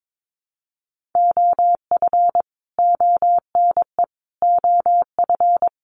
Morse code, heard as a keyed tone, about 22 wpm, 700 Hz. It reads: OF ONE OF
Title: Sets of 4 Words 22wpm